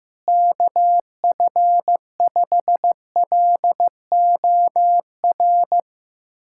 Morse code, heard as a keyed tone, about 15 wpm, 700 Hz. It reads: KF5LOR
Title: USA Callsign DE Callsign 15wpm